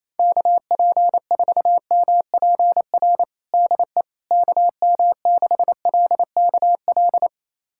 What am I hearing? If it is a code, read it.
Morse code, heard as a keyed tone, about 28 wpm, 700 Hz. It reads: KP4MPR DE KM6LKL
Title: USA Callsign DE Callsign - 6 Characters - 28wpm